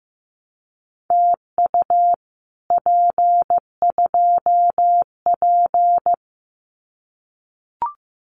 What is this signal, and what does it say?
Morse code, heard as a keyed tone, about 15 wpm, 700 Hz. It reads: TU P2P
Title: Common QSO Phrases Encore 15wpm